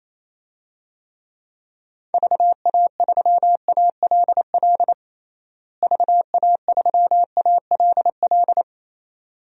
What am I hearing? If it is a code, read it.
Morse code, heard as a keyed tone, about 28 wpm, 700 Hz. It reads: VA3ALL VA3ALL
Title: DXCC-100 Call Signs - 6 Characters - Repeated 2x - 28wpm